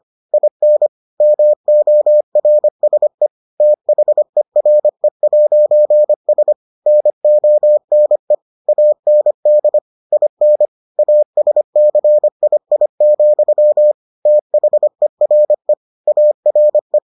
Morse code, heard as a keyed tone, about 25 wpm, 600 Hz. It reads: IN MORSE THERE'S NONE AND IN ASCII, THERE ARE